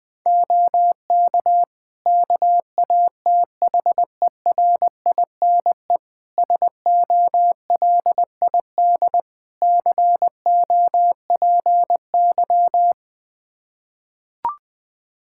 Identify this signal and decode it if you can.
Morse code, heard as a keyed tone, about 20 wpm, 700 Hz. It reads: OK KATHERINE SOLID COPY
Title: Practice QSO - Exchange 3 - More Info - 20wpm